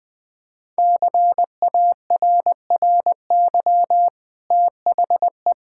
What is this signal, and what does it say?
Morse code, heard as a keyed tone, about 20 wpm, 700 Hz. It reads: CARRY THE
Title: Sentences from Top 1600 Words 20wpm